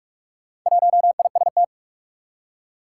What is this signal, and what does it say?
Morse code, heard as a keyed tone, about 46 wpm, 700 Hz. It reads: 1IST
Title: Four Challenging Letter-Number Combinations 45wpm